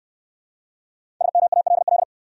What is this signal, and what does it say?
Morse code, heard as a keyed tone, about 67 wpm, 700 Hz. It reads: SMALL